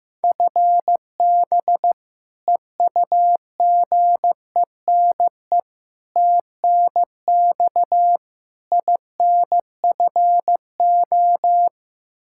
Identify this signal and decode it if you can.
Morse code, heard as a keyed tone, about 15 wpm, 700 Hz. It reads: FB EUGENE TNX INFO
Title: Practice QSO - Exchange 3 - More Info - 15wpm